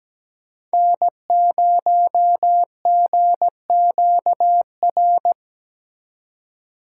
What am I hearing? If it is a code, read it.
Morse code, heard as a keyed tone, about 20 wpm, 700 Hz. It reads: N0GQR
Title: Speed Racing - USA 1x3 Call Signs 17wpm